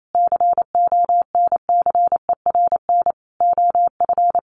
Morse code, heard as a keyed tone, about 28 wpm, 700 Hz. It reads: CONCERN OF
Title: Sentences from Top 1000 Words 28wpm